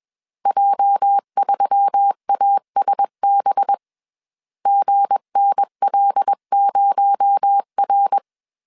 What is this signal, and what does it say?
Morse code, heard as a keyed tone, about 21 wpm, 800 Hz. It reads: J3ASB GNL0R
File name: CW / DET / 1 9 1500C S C 20wpm 2310